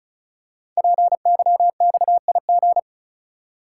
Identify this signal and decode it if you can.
Morse code, heard as a keyed tone, about 36 wpm, 700 Hz. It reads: PYXIG
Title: Code Groups - 5 Letters - 35wpm